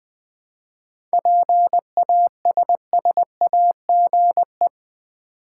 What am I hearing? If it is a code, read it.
Morse code, heard as a keyed tone, about 20 wpm, 700 Hz. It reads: PASSAGE
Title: 7 Letter Words - First 3 Letters Given - 20wpm